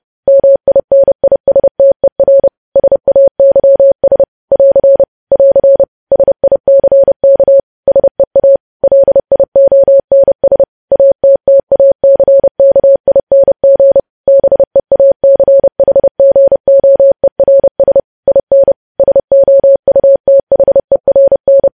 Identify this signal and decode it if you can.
Morse code, heard as a keyed tone, about 30 wpm, 550 Hz. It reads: MINISTER SAYS <AR> <AR> SICK SEA LIONS ATTACKING BEACHGOERS IN SOUTHERN
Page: news headlines in morse code at 30 wpm